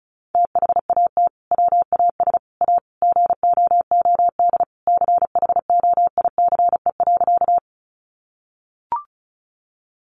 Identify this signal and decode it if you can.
Morse code, heard as a keyed tone, about 34 wpm, 700 Hz. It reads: THAT WAS A GOOD CHOICE.
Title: Sentences from Top 900 Words 35wpm